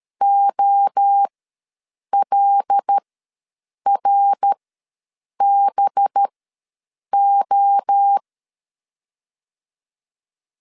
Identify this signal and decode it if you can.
Morse code, heard as a keyed tone, about 13 wpm, spaced as for 7 wpm, 800 Hz. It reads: OLRBO
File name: LEZIONI / 5 LBVC / 5 LBV ws 12wpm 3015